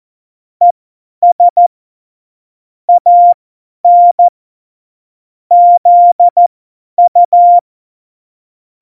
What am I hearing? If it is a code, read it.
Morse code, heard as a keyed tone, about 14 wpm, spaced as for 10 wpm, 700 Hz. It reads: ES AN ZU
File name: Morsekurs / Witze / Witz 14. August 2018, 50 F70